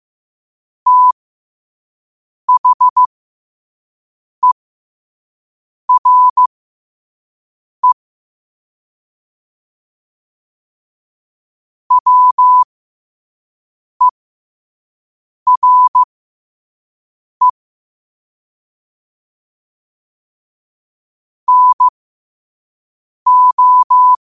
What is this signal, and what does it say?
Morse code, heard as a keyed tone, about 15 wpm, spaced as for 5 wpm, 1000 Hz. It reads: THERE WERE NO
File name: qotdfiles / 5wpm